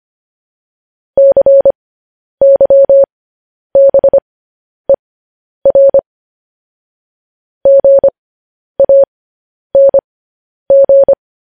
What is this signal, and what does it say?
Morse code, heard as a keyed tone, about 25 wpm, spaced as for 10 wpm, 550 Hz. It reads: CYBER GANG